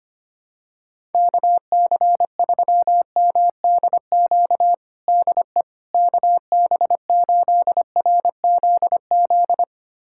Morse code, heard as a keyed tone, about 25 wpm, 700 Hz. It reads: KC3MDQ DE KB8RZZ
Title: USA Callsign DE Callsign 25wpm